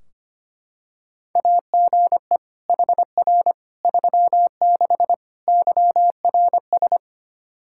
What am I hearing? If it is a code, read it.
Morse code, heard as a keyed tone, about 25 wpm, 700 Hz. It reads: AGE HR 36 YRS
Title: Practice QSO - Exchange 3 - More Info - 25wpm